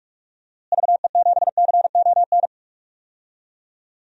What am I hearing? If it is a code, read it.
Morse code, heard as a keyed tone, about 44 wpm, 700 Hz. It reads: VE7CON
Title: DXCC-100 Call Signs - 6 Characters - Repeated 2x - 45wpm